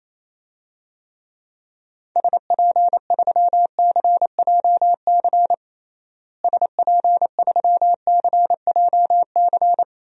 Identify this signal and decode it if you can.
Morse code, heard as a keyed tone, about 28 wpm, 700 Hz. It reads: SP3CJC SP3CJC
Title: DXCC-100 Call Signs - 6 Characters - Repeated 3x - 28wpm